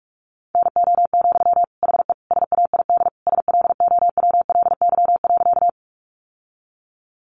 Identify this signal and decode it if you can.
Morse code, heard as a keyed tone, about 44 wpm, 700 Hz. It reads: NO, HE SAID SLOWLY.
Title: Sentences from Top 1100 Words 45wpm